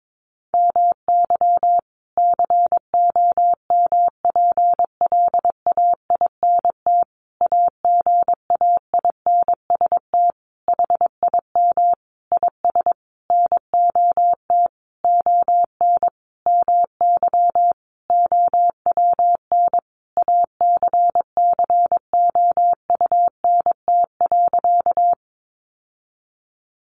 Morse code, heard as a keyed tone, about 22 wpm, 700 Hz. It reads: MY COMPLAINT AGAINST HIM IS NOT ON MY OWN ACCOUNT.
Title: Sentences from Top 1400 Words 22wpm